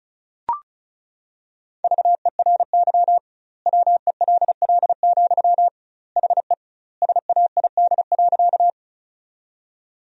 Morse code, heard as a keyed tone, about 36 wpm, 700 Hz. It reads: VERY WELL, HE SAID.
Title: 2x Word-Spacing - Sentences from Top 200 Words - 35wpm